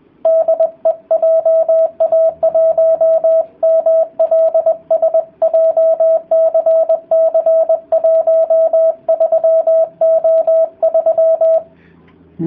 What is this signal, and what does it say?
Morse code, heard as a keyed tone, about 20 wpm, 650 Hz, with radio noise. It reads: DE JA1MLS JCC 13O3